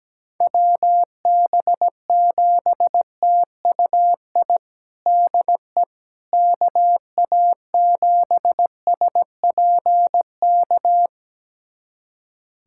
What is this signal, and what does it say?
Morse code, heard as a keyed tone, about 17 wpm, 700 Hz. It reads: WB7TUI DE KA7SPK
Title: USA Callsign DE Callsign 17wpm